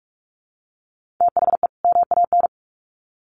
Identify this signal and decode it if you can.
Morse code, heard as a keyed tone, about 44 wpm, 700 Hz. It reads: THE MAN
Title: Sets of 2 Words 45wpm